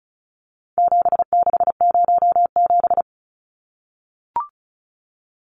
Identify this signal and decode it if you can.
Morse code, heard as a keyed tone, about 34 wpm, 700 Hz. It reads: 7607
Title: Numbers - 4-Digit 35wpm